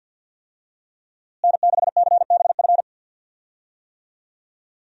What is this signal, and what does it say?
Morse code, heard as a keyed tone, about 51 wpm, 700 Hz. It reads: N6CBF
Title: USA Call Signs - 5 Characters - Repeated 2x - 50wpm